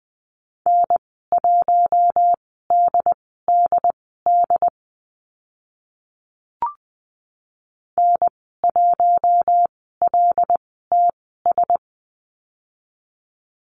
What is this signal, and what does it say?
Morse code, heard as a keyed tone, about 20 wpm, spaced as for 14 wpm, 700 Hz. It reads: N1DDD N1LTS
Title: 076 Lesson 13g - Letter L - Call Signs 20wpm